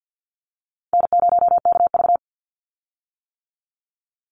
Morse code, heard as a keyed tone, about 50 wpm, 700 Hz. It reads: N0KV
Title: Easy Call Signs - 50wpm